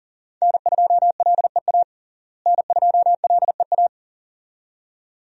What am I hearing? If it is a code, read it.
Morse code, heard as a keyed tone, about 37 wpm, 700 Hz. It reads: N2LEA N2LEA
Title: USA Call Signs - 5 Characters - Repeated 2x - 40wpm